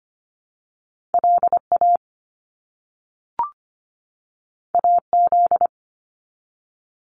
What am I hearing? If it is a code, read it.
Morse code, heard as a keyed tone, about 25 wpm, 700 Hz. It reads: LA AZ
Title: US States 25wpm